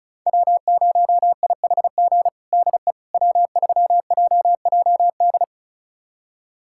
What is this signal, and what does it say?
Morse code, heard as a keyed tone, about 36 wpm, 700 Hz. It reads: W0IHG DE W3JJD
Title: USA Callsign DE Callsign - 5 Characters - 35wpm